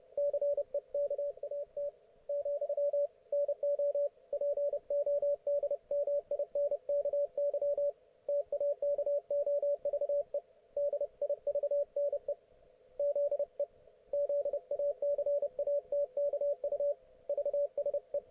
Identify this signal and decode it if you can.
Morse code, heard as a keyed tone, about 30 wpm, 570 Hz, with radio noise. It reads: CEKAT , NO PODMINKY TAKOVE DIVNE ZE ZACATKU VSE